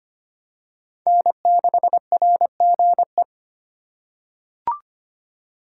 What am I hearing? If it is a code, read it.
Morse code, heard as a keyed tone, about 25 wpm, 700 Hz. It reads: N6RGE
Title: USA Call Signs - 4x Character Spacing - 25wpm